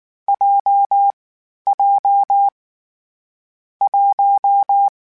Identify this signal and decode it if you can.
Morse code, heard as a keyed tone, about 19 wpm, 800 Hz. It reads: J J 1
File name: Difficult Combos / Difficult Combos-mp3s / WPJ WPJ1 @18